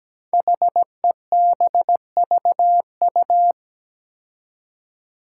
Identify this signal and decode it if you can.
Morse code, heard as a keyed tone, about 17 wpm, 700 Hz. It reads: HEBVU
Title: Five Challenging Letter-Number Combinations 17wpm